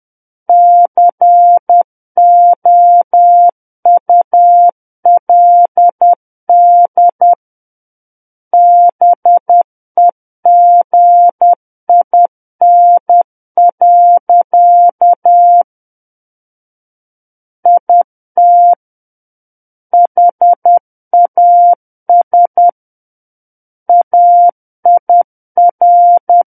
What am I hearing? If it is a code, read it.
Morse code, heard as a keyed tone, about 10 wpm, 700 Hz. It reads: COULD BEGIN. IT HAS AIR
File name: Book One / 01-Chapter 1 - 10 WPM